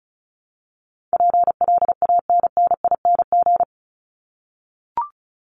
Morse code, heard as a keyed tone, about 34 wpm, 700 Hz. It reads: PLANNING
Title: ING Suffix - Top 100 Words with Base Words 35wpm